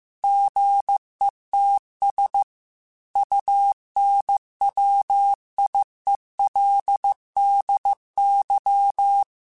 Morse code, heard as a keyed tone, about 15 wpm, 800 Hz. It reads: GETS UNWIELDY
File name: Technical Subjects / Technical Subjects-mp3s / AND Gate @13